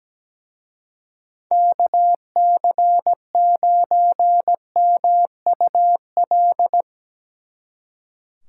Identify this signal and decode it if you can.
Morse code, heard as a keyed tone, about 17 wpm, 700 Hz. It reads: KC9MUL
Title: Speed Racing - USA 2x3 Call Signs 17wpm